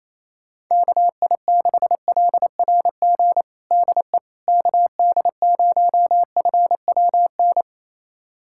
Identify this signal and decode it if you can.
Morse code, heard as a keyed tone, about 28 wpm, 700 Hz. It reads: KI6LRG DE KD0FWN
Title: USA Callsign DE Callsign - 6 Characters - 28wpm